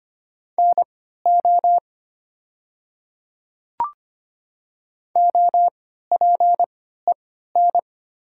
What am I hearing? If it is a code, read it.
Morse code, heard as a keyed tone, about 25 wpm, spaced as for 14 wpm, 700 Hz. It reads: NO OPEN
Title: 3x Character Spacing - Top 300 Words - 25wpm